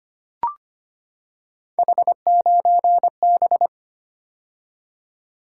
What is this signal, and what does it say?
Morse code, heard as a keyed tone, about 25 wpm, 700 Hz. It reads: H9B
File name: Three Letter-Number Combinations - 25wpm